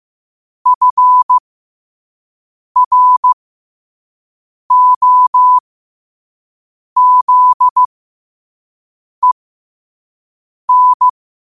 Morse code, heard as a keyed tone, about 15 wpm, spaced as for 5 wpm, 1000 Hz. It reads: FROZEN
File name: qotdfiles / 5wpm